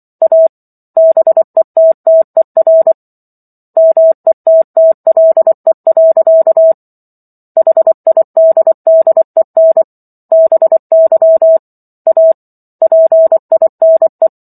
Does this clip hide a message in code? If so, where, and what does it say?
Yes: Morse code from 0.2 to 14.3 s, heard as a keyed tone, about 24 wpm, 650 Hz. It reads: A BETTER METTLE. HIDDEN BY A PINE